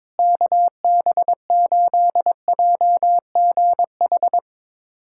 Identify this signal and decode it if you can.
Morse code, heard as a keyed tone, about 22 wpm, 700 Hz. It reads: KB8JGH
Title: USA Callsign DE Callsign - 6 Characters - 22wpm